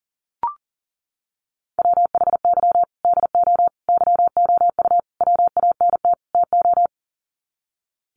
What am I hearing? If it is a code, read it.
Morse code, heard as a keyed tone, about 40 wpm, 700 Hz. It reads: WHY DO YOU WANT TO
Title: Sets of 5 Words Encore2 40wpm